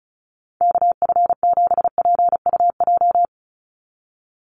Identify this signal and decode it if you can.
Morse code, heard as a keyed tone, about 34 wpm, 700 Hz. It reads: KF7PUJ
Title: USA Call Signs 35wpm